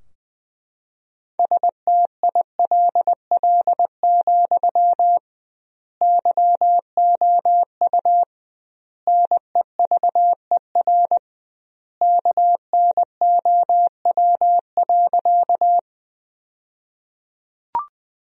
Morse code, heard as a keyed tone, about 20 wpm, 700 Hz. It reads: STILL, YOU NEVER KNOW.
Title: 2x Word-Spacing - Sentences from Top 300 Words - 20wpm